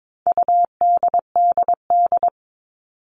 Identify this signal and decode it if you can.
Morse code, heard as a keyed tone, about 22 wpm, 700 Hz. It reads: UDDD
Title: All Cut Numbers - 4-Digit 22wpm